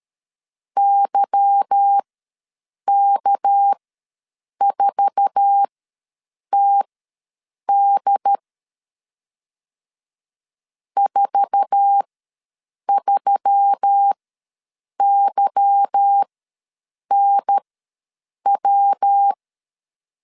Morse code, heard as a keyed tone, about 13 wpm, spaced as for 7 wpm, 800 Hz. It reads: YK4TD 43YNW
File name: LEZIONI / 8 YJ45 / 8 YJ4 ws 12wpm 3249